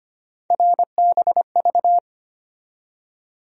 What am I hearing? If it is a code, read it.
Morse code, heard as a keyed tone, about 25 wpm, 700 Hz. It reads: RBV
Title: USA Call Sign Postfixes - 3 Letters 25wpm